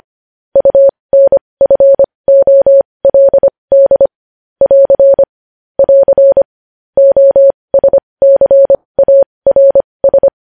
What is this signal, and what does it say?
Morse code, heard as a keyed tone, about 25 wpm, spaced as for 20 wpm, 550 Hz. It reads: UNFOLD <AR> <AR> OSCARS
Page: News Headlines in Morse Code at 20 WPM